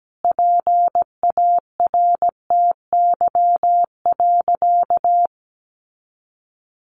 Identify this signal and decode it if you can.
Morse code, heard as a keyed tone, about 17 wpm, 700 Hz. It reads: PARTY.
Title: Sentences from Top 1400 Words 17wpm